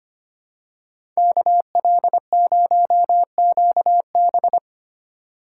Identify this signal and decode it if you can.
Morse code, heard as a keyed tone, about 25 wpm, 700 Hz. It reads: KL0QB
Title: USA Call Signs - 2x Character Spacing - 25wpm